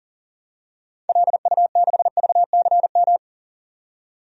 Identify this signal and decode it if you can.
Morse code, heard as a keyed tone, about 37 wpm, 700 Hz. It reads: LU6VCM
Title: DXCC-100 Call Signs - 6 Characters - 40wpm